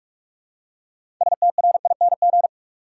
Morse code, heard as a keyed tone, about 46 wpm, 700 Hz. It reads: STRING
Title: Six Letter Words - 45wpm